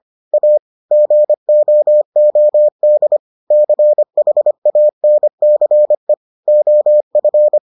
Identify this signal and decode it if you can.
Morse code, heard as a keyed tone, about 25 wpm, 600 Hz. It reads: A GOOD CHANCE OF